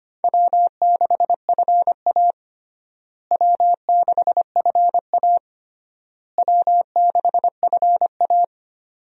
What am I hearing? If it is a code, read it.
Morse code, heard as a keyed tone, about 25 wpm, 700 Hz. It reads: W6FA W6FA W6FA
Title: USA Call Signs - 4 Characters - Repeated 3x - 25wpm